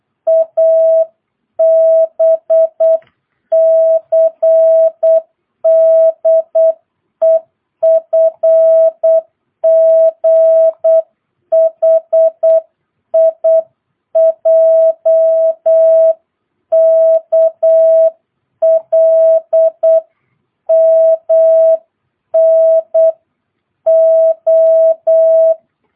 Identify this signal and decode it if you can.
Morse code, heard as a keyed tone, about 8 wpm, 650 Hz. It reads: ABCDEFGHIJKLMNO
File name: jh8cbh / 2022cw